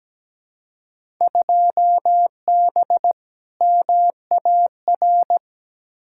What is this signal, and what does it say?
Morse code, heard as a keyed tone, about 17 wpm, 700 Hz. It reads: 2B MAR
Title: Field Day Exchanges Encore - 17wpm